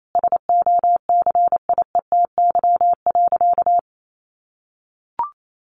Morse code, heard as a keyed tone, about 28 wpm, 700 Hz. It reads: SOCIETY.
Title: Sentences from Top 700 Words 28wpm